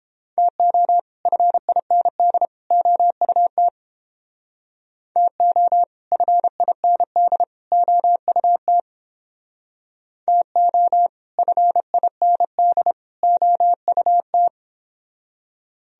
Morse code, heard as a keyed tone, about 30 wpm, 700 Hz. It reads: TO FIND OUT TO FIND OUT TO FIND OUT